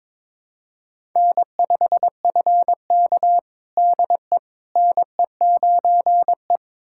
Morse code, heard as a keyed tone, about 21 wpm, 700 Hz. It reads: N5FK DE NE9E
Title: USA Callsign DE Callsign - 4 Characters - 22wpm